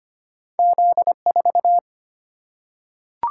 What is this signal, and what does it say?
Morse code, heard as a keyed tone, about 25 wpm, 700 Hz. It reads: Z4
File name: Two Letter-Number Combinations - 25wpm